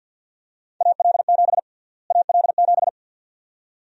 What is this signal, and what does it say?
Morse code, heard as a keyed tone, about 52 wpm, 700 Hz. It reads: AL7 AL7
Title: USA Call Sign Prefixes - Repeated 2x - 50wpm